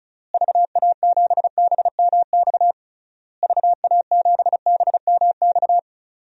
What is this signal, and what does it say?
Morse code, heard as a keyed tone, about 34 wpm, 700 Hz. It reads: VA7BMX VA7BMX
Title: DXCC-100 Call Signs - 6 Characters - Repeated 3x - 35wpm